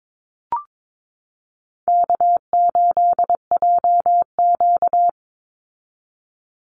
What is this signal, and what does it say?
Morse code, heard as a keyed tone, about 22 wpm, 700 Hz. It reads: K8JQ
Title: Easy Call Signs Repeated 3x - 22wpm